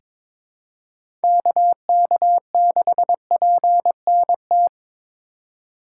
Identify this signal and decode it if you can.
Morse code, heard as a keyed tone, about 22 wpm, 700 Hz. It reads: KK6PNT
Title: USA Call Signs - 2x Character Spacing - 22wpm